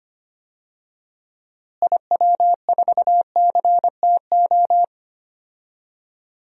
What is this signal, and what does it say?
Morse code, heard as a keyed tone, about 25 wpm, 700 Hz. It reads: IW4CTO
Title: DXCC-100 Call Signs - 6 Characters - Repeated 3x - 25wpm